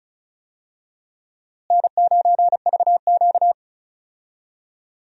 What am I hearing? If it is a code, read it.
Morse code, heard as a keyed tone, about 34 wpm, 700 Hz. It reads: N9VQ
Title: USA Call Signs - 4 Characters - Repeated 3x - 35wpm